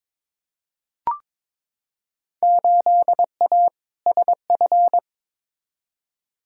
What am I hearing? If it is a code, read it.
Morse code, heard as a keyed tone, about 21 wpm, 700 Hz. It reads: 8A SF
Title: Field Day Exchanges Encore - 22wpm